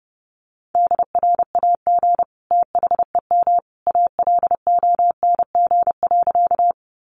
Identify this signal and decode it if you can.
Morse code, heard as a keyed tone, about 30 wpm, 700 Hz. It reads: DRAG THEM ALONG.
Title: Sentences from Top 1500 Words 30wpm